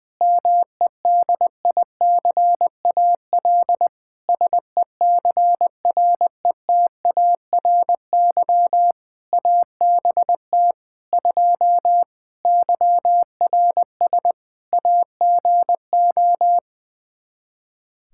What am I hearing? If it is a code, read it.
Morse code, heard as a keyed tone, about 20 wpm, 700 Hz. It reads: MEDICAL SECRETARY ABT 2 YRS AGO
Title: Practice QSO - Exchange 3 - More Info - 20wpm